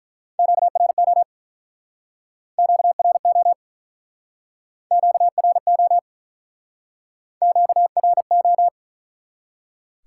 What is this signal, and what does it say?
Morse code, heard as a keyed tone, about 48 wpm, 700 Hz. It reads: QRO QRO QRO QRO